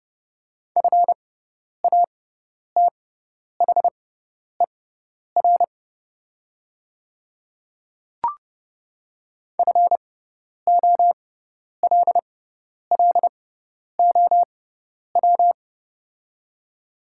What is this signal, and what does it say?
Morse code, heard as a keyed tone, about 30 wpm, spaced as for 10 wpm, 690 Hz. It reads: FATHER FOLLOW
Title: 6x Character Spacing - Top 300 Words - 30wpm